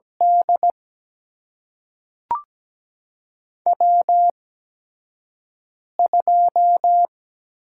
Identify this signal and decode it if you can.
Morse code, heard as a keyed tone, about 17 wpm, spaced as for 5 wpm, 700 Hz. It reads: D W2
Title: USA Call Signs - 4 Characters - 8x Character Spacing - 17wpm